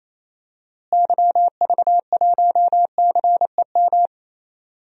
Morse code, heard as a keyed tone, about 28 wpm, 700 Hz. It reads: YV1CEM
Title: DXCC-100 Call Signs - 6 Characters - Repeated 3x - 28wpm